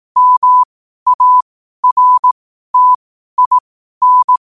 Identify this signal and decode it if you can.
Morse code, heard as a keyed tone, about 18 wpm, spaced as for 12 wpm, 1000 Hz. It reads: MARTIN
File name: qotdfiles / 12wpm